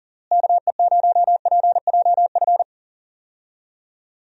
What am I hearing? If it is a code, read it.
Morse code, heard as a keyed tone, about 40 wpm, 700 Hz. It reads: KE0PJF
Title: USA Call Signs - 6 Characters - Repeated 3x - 40wpm